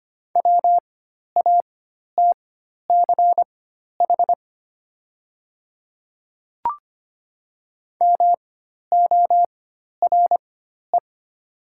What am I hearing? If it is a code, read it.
Morse code, heard as a keyed tone, about 25 wpm, spaced as for 12 wpm, 700 Hz. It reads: WATCH MORE
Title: 4x Character Spacing - Top 300 Words - 25wpm